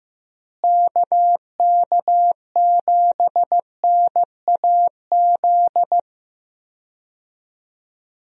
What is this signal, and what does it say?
Morse code, heard as a keyed tone, about 15 wpm, 700 Hz. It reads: KK7NAZ
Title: USA Call Signs - 6 Characters - Repeated 3x - 15wpm